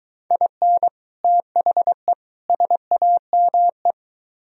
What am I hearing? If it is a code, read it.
Morse code, heard as a keyed tone, about 24 wpm, 700 Hz. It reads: IN THE SAME